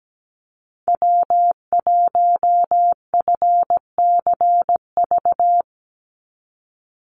Morse code, heard as a keyed tone, about 17 wpm, 700 Hz. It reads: W1FCV
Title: Call Signs - 17wpm